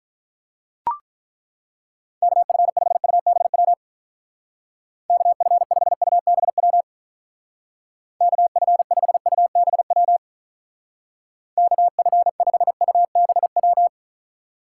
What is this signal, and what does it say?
Morse code, heard as a keyed tone, about 42 wpm, 700 Hz. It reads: KF5UBW KF5UBW KF5UBW KF5UBW
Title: Speed Racing - USA 2x3 Call Signs 35wpm